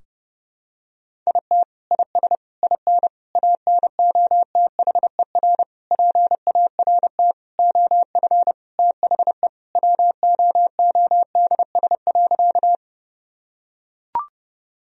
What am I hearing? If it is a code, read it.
Morse code, heard as a keyed tone, about 32 wpm, 700 Hz. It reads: IT IS IN ANOTHER PART OF THE WOODS.
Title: Sentences from Top 1000 Words - Review Part 2 - 30wpm